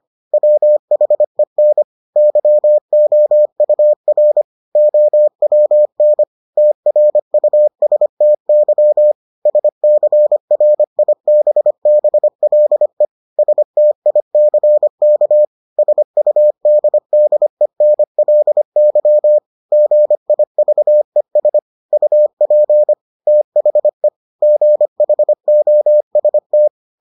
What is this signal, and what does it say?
Morse code, heard as a keyed tone, about 25 wpm, 600 Hz. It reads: WHEN YOUR OWN TRUSTY SCRIBBLE STICK SUDDENLY GIVES UP THE GHOST